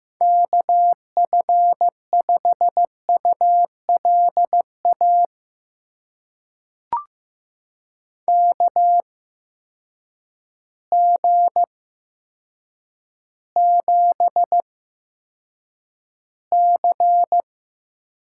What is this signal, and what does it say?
Morse code, heard as a keyed tone, about 15 wpm, spaced as for 8 wpm, 700 Hz. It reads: KF5ULA KG7C
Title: USA Call Signs - 6 Characters - 8x Character Spacing - 15wpm